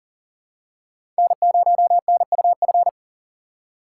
Morse code, heard as a keyed tone, about 40 wpm, 700 Hz. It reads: N0NUF
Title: USA Call Signs - 5 Characters - 3x Character Spacing - 40wpm